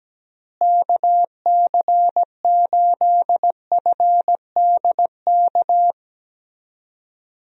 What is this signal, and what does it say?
Morse code, heard as a keyed tone, about 17 wpm, 700 Hz. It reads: KC8FDK
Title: USA Call Signs - 6 Characters - Repeated 3x - 17wpm